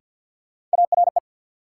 Morse code, heard as a keyed tone, about 54 wpm, 700 Hz. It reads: ARE
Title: Sentences from Top 100 Words - New Word Drill - 50wpm